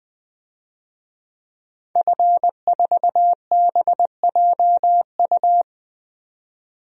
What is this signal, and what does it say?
Morse code, heard as a keyed tone, about 20 wpm, 700 Hz. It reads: F4BJU
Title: DXCC-100 Call Signs - 5 Characters - 20wpm